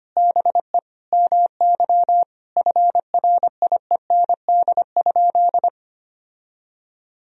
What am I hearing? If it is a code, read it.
Morse code, heard as a keyed tone, about 25 wpm, 700 Hz. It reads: BE MY FRIEND?
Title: Sentences from Top 1000 Words - Review Part 3 - 25wpm